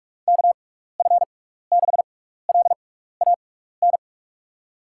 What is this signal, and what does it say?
Morse code, heard as a keyed tone, about 46 wpm, 700 Hz. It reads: K F 6 L A N
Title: USA Call Signs - 6 Characters - 6x Character Spacing - 45wpm